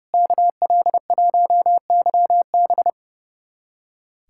Morse code, heard as a keyed tone, about 30 wpm, 700 Hz. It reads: KL1YB
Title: USA Callsign DE Callsign - 5 Characters - 30wpm